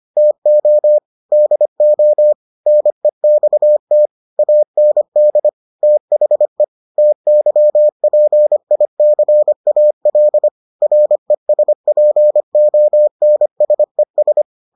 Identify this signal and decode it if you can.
Morse code, heard as a keyed tone, about 25 wpm, 600 Hz. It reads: TO DO NEXT AND THE TYPICAL RESPONSES